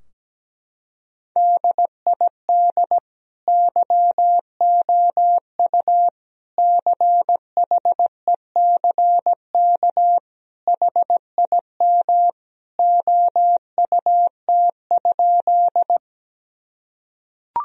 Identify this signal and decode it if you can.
Morse code, heard as a keyed tone, about 17 wpm, 700 Hz. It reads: DID YOU CHECK HIM OUT?
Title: Sentences from Top 1000 Words - Review Part 3 - 17wpm